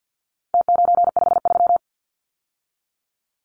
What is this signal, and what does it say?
Morse code, heard as a keyed tone, about 50 wpm, 700 Hz. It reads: T953
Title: Numbers - Contest Style - 4-Digit 50wpm